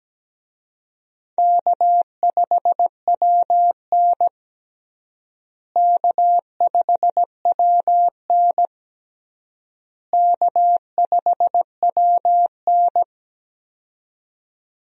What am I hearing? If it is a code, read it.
Morse code, heard as a keyed tone, about 17 wpm, 700 Hz. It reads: K5WN K5WN K5WN
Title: USA Call Signs - 4 Characters - Repeated 3x - 17wpm